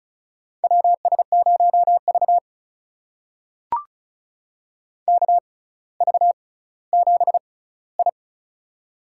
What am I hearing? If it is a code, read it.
Morse code, heard as a keyed tone, about 36 wpm, spaced as for 18 wpm, 700 Hz. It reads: WS0V KV7I
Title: USA Call Signs - 4 Characters - 6x Character Spacing - 35wpm